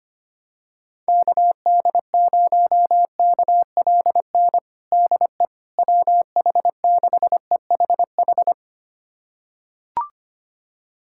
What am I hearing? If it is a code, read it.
Morse code, heard as a keyed tone, about 25 wpm, 700 Hz. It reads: KD0KLN DE WH6EHH
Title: USA Callsign DE Callsign - 6 Characters - 25wpm